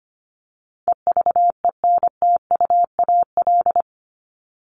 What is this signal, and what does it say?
Morse code, heard as a keyed tone, about 25 wpm, 700 Hz. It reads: EVENTUAL
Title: LY Suffix - Top 100 Words with Base Words 25wpm